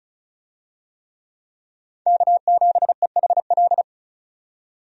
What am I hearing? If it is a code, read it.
Morse code, heard as a keyed tone, about 34 wpm, 700 Hz. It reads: K7EHL
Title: USA Call Signs - 5 Characters - 35wpm